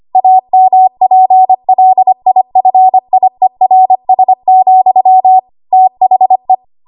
Morse code, heard as a keyed tone, about 25 wpm, 750 Hz. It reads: AMPLIFIERS, THE
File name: Archive / 25 WPM / 25WPM